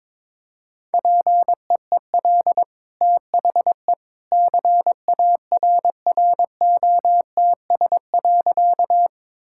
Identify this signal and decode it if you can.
Morse code, heard as a keyed tone, about 22 wpm, 690 Hz. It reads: PEEL THE CARROTS.
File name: practice / 22wpm / Easy Sentences with Hints - 22wpm